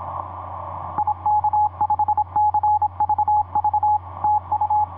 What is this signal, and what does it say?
Morse code, heard as a keyed tone, about 26 wpm, 840 Hz. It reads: IK5CVV TU
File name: pz5ra ik5cvv etc 40m cw 14jan2010 2310